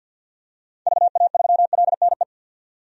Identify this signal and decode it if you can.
Morse code, heard as a keyed tone, about 52 wpm, 700 Hz. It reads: VA3LNE
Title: DXCC-100 Call Signs - 6 Characters - 50wpm